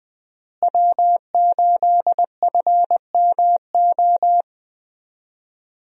Morse code, heard as a keyed tone, about 20 wpm, 700 Hz. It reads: W8FMO
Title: USA Call Signs - 5 Characters - Repeated 2x - 20wpm